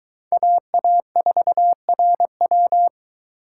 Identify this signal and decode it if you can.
Morse code, heard as a keyed tone, about 22 wpm, 700 Hz. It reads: AA4RW